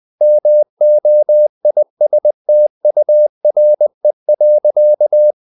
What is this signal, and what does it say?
Morse code, heard as a keyed tone, about 20 wpm, 600 Hz. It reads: MOISTURE.
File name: the old man and the sea part 2 20wpm